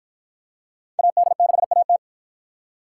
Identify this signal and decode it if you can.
Morse code, heard as a keyed tone, about 48 wpm, 700 Hz. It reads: AD6AT